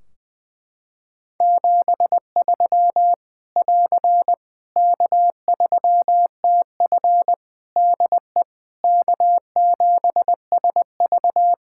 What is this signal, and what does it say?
Morse code, heard as a keyed tone, about 20 wpm, 700 Hz. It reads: 73 <AR> K3TF DE K7SV
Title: Practice QSO - Exchange 4 - Ending - 20wpm